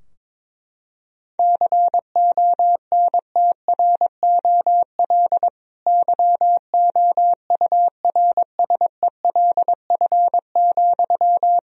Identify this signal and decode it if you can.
Morse code, heard as a keyed tone, about 23 wpm, 700 Hz. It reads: CONTROL YOURSELF,
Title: Sentences from Top 1000 Words - Review Part 3 - 22wpm